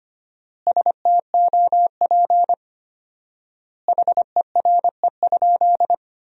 Morse code, heard as a keyed tone, about 25 wpm, 700 Hz. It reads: STOP HERE?
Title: 4x Word-Spacing - Sentences from Top 300 Words - 25wpm